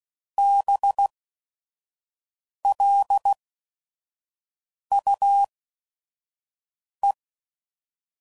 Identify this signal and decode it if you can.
Morse code, heard as a keyed tone, about 16 wpm, spaced as for 5 wpm, 800 Hz. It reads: BLUE
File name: Suffixes / Special ING's / VerySlowToFaster ING's